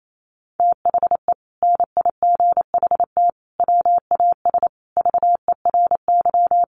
Morse code, heard as a keyed tone, about 28 wpm, 700 Hz. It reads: THE NIGHT WAS VERY
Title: Sentences from Top 500 Words 28wpm